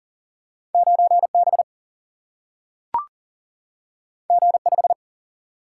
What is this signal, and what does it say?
Morse code, heard as a keyed tone, about 39 wpm, 700 Hz. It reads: 9 B G 5
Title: Single-Letter Single-Number 40wpm